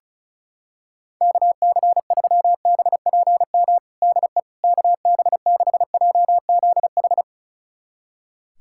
Morse code, heard as a keyed tone, about 34 wpm, 700 Hz. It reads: KC3BPM DE KB6JZH
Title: USA Callsign DE Callsign - 6 Characters - 35wpm